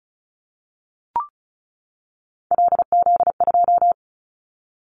Morse code, heard as a keyed tone, about 34 wpm, 700 Hz. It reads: LZ2
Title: Callsign Prefixes for Top 25 DXCC - 35wpm